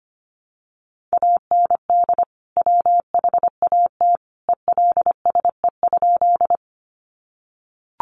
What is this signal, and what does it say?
Morse code, heard as a keyed tone, about 25 wpm, 700 Hz. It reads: AND WHAT ELSE?
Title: Sentences from Top 800 Words 25wpm